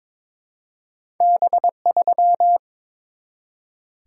Morse code, heard as a keyed tone, about 22 wpm, 700 Hz. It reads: B3
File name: practice / 22wpm / Two Letter-Number Combinations - 22wpm